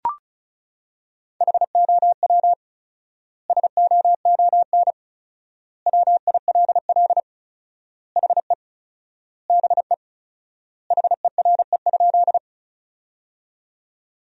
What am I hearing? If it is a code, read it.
Morse code, heard as a keyed tone, about 36 wpm, 700 Hz. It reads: HOW SOON WILL HE BE HERE?
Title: 4x Word-Spacing - Sentences from Top 400 Words - 35wpm